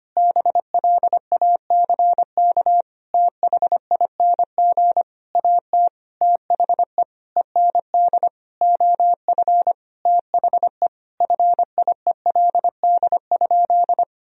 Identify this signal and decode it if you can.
Morse code, heard as a keyed tone, about 25 wpm, 700 Hz. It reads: BLACK THING AT THE END OF THE FIELD?
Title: Sentences from Top 1000 Words - Review Part 2 - 25wpm